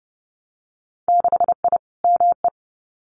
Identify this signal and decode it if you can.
Morse code, heard as a keyed tone, about 30 wpm, 700 Hz. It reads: 6I ME
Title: Winter Field Day Exchanges Encore 30wpm